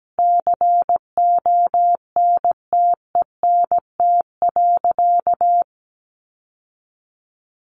Morse code, heard as a keyed tone, about 17 wpm, 700 Hz. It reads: CONTENT.
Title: Sentences from Top 1400 Words 17wpm